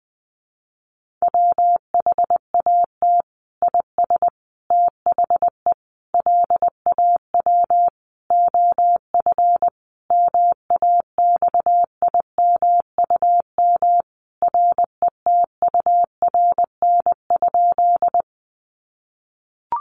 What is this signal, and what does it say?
Morse code, heard as a keyed tone, about 20 wpm, 700 Hz. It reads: WHAT IS THE LAW OF MAXIMUM RETURN?
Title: Sentences from Top 1300 Words 20wpm